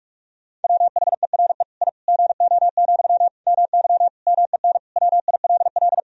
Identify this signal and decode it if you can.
Morse code, heard as a keyed tone, about 46 wpm, 700 Hz. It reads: WHERE I GO, MY MEN WILL
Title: Sentences from Top 1000 Words - Review Part 1 - 45wpm